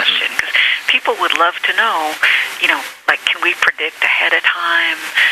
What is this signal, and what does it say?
Emergency Alert System (SAME) tones, heard as harsh buzzing bursts, with speech. It is heard from 0.3 to 4.4 s.